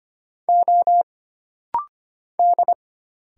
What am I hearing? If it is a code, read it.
Morse code, heard as a keyed tone, about 25 wpm, 700 Hz. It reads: O D
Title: Single Letters - Rapid-Fire 25wpm